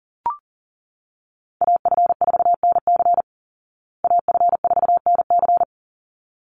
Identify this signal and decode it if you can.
Morse code, heard as a keyed tone, about 40 wpm, 700 Hz. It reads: AF4NC AF4NC
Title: Call Signs Repeated 3x - 40wpm